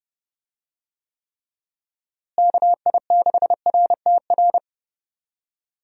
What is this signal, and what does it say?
Morse code, heard as a keyed tone, about 30 wpm, 700 Hz. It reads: KI6RTR
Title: USA Call Signs - 6 Characters - Repeated 2x - 30wpm